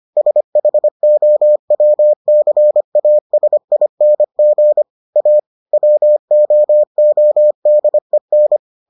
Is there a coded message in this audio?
Yes: Morse code from 0.2 to 8.6 s, heard as a keyed tone, about 25 wpm, 600 Hz. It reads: SHOWCASING A WOODEN